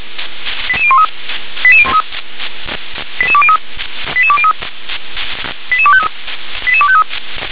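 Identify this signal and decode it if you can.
Five-tone selective call (ZVEI), heard as quick runs of five tones, tones of about 70 ms. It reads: 88013 88203 88203 88253 88253
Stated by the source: ZVEI five-tone selective call received by SDR.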